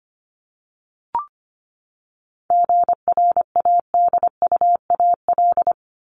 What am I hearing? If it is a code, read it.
Morse code, heard as a keyed tone, about 25 wpm, 700 Hz. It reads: GRADUAL
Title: LY Suffix - Top 100 Words with Base Words 25wpm